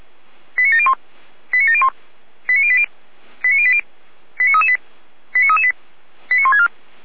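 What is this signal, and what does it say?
Five-tone selective call (ZVEI), heard as quick runs of five tones, tones of about 80 ms. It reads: 89701 89701 89080 89080 89338 89338 89164
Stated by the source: Five-tone ZVEI selective call 89700, received by SDR.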